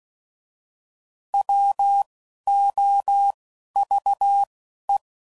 Morse code, heard as a keyed tone, about 16 wpm, 800 Hz. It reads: W O V E
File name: Prefixes / By First Letter / Ws